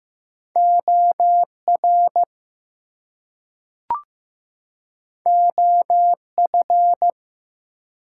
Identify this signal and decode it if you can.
Morse code, heard as a keyed tone, about 15 wpm, 700 Hz. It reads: OR OF
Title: Two Letter Words - 15wpm